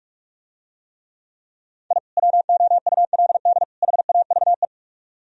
Morse code, heard as a keyed tone, about 49 wpm, 700 Hz. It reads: I WOULD HAVE